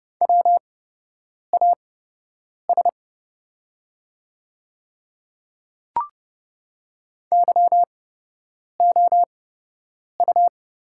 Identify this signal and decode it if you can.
Morse code, heard as a keyed tone, about 30 wpm, spaced as for 8 wpm, 700 Hz. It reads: WAS YOU
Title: 8x Character Spacing - Top 100 Words - 30wpm